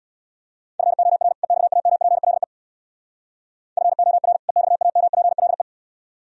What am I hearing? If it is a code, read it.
Morse code, heard as a keyed tone, about 63 wpm, 700 Hz. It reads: FOR EXAMPLE FOR EXAMPLE